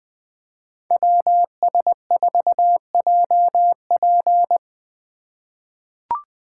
Morse code, heard as a keyed tone, about 20 wpm, 700 Hz. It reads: WS4JP
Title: USA Call Signs - 5 Characters - 4x Character Spacing - 20wpm